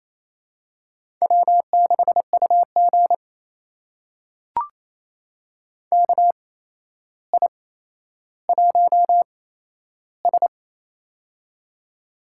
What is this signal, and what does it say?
Morse code, heard as a keyed tone, about 28 wpm, spaced as for 12 wpm, 700 Hz. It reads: W6UG KI1S
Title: USA Call Signs - 4 Characters - 8x Character Spacing - 28wpm